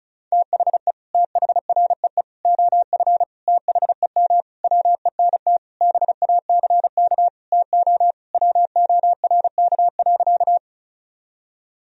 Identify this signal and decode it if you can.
Morse code, heard as a keyed tone, about 35 wpm, 700 Hz. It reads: THE THREE OF THEM WENT BACK TO WORK.
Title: Sentences from Top 1000 Words - Review Part 1 - 35wpm